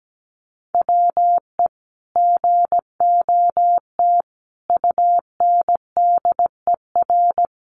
Morse code, heard as a keyed tone, about 17 wpm, 700 Hz. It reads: WE GOT UNDER
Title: Sentences from Top 300 Words 17wpm